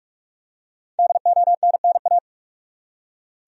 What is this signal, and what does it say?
Morse code, heard as a keyed tone, about 46 wpm, 700 Hz. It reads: DONNA
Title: USA Common Names - Max 5 Letters - 45wpm